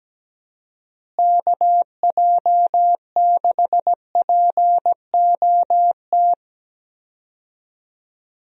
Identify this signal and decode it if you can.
Morse code, heard as a keyed tone, about 17 wpm, 700 Hz. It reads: KJ6POT
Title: USA Call Signs - 6 Characters - Repeated 2x - 17wpm